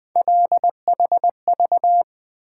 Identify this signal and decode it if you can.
Morse code, heard as a keyed tone, about 20 wpm, 700 Hz. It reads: LHV
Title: Three Letter-Number Combinations - 20wpm